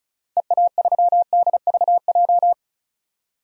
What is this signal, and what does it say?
Morse code, heard as a keyed tone, about 34 wpm, 700 Hz. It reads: EA3DVJ
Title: DXCC-100 Call Signs - 6 Characters - 35wpm